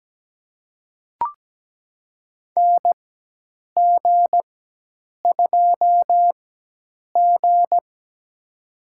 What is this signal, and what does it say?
Morse code, heard as a keyed tone, about 17 wpm, spaced as for 8 wpm, 700 Hz. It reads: NG2G
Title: USA Call Signs - 4 Characters - 4x Character Spacing - 17wpm